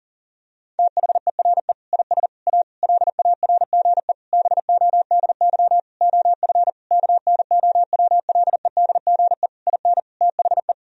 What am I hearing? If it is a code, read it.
Morse code, heard as a keyed tone, about 37 wpm, 700 Hz. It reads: THERE IS A LARGE BODY OF KNOWLEDGE IN THE
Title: Sentences from Top 1000 Words - Review Part 4 - 40wpm